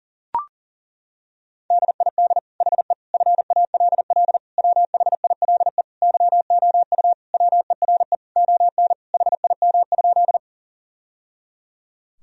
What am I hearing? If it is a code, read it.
Morse code, heard as a keyed tone, about 40 wpm, 700 Hz. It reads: DID HE FALL WHILE YOU WERE ON HIM?
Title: Sentences from Top 1000 Words - Review Part 2 - 40wpm